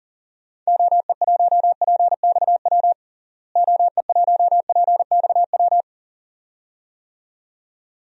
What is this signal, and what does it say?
Morse code, heard as a keyed tone, about 37 wpm, 700 Hz. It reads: OE1PXW OE1PXW
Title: DXCC-100 Call Signs - 6 Characters - Repeated 2x - 40wpm